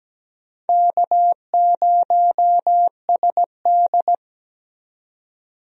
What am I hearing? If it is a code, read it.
Morse code, heard as a keyed tone, about 17 wpm, 700 Hz. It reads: K0SD
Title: USA Call Signs - 4 Characters - 17wpm